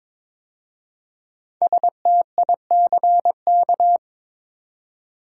Morse code, heard as a keyed tone, about 22 wpm, 700 Hz. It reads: STICK
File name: 5 Letter Words - First 3 Letters Given - 22wpm